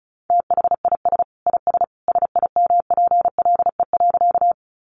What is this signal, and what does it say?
Morse code, heard as a keyed tone, about 34 wpm, 700 Hz. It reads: THIS IS SIMPLE.
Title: Sentences from Top 400 Words 35wpm